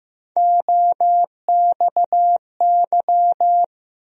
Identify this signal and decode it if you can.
Morse code, heard as a keyed tone, about 15 wpm, 700 Hz. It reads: OXY
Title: USA Call Sign Postfixes - 3 Letters 15wpm